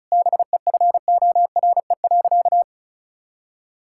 Morse code, heard as a keyed tone, about 36 wpm, 700 Hz. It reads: BEFORE.
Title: Sentences from Top 1000 Words - Review Part 5 - 35wpm